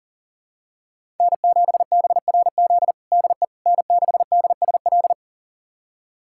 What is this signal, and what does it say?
Morse code, heard as a keyed tone, about 40 wpm, 700 Hz. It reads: N7BRZ DE N6DSL
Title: USA Callsign DE Callsign - 5 Characters - 40wpm